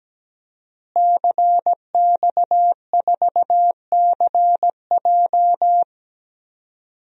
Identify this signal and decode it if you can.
Morse code, heard as a keyed tone, about 17 wpm, 700 Hz. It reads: CX4CJ
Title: DXCC-100 Call Signs - 5 Characters - Repeated 3x - 17wpm